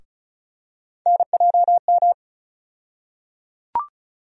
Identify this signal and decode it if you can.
Morse code, heard as a keyed tone, about 36 wpm, 700 Hz. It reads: NJM
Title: USA Call Sign Postfixes - 3 Letters 35wpm